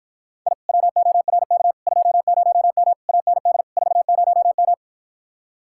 Morse code, heard as a keyed tone, about 53 wpm, 700 Hz. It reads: I WORK 20M AND 40M